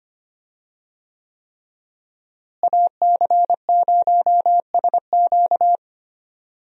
Morse code, heard as a keyed tone, about 25 wpm, 700 Hz. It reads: AC0SQ
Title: USA Call Signs - 5 Characters - Repeated 3x - 25wpm